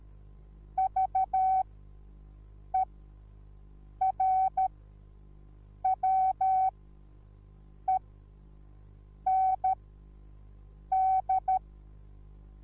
Morse code, heard as a keyed tone, about 13 wpm, spaced as for 6 wpm, 760 Hz. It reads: VERWEND
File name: Onlinelehrgang / morsen / DJ4UF Morsen21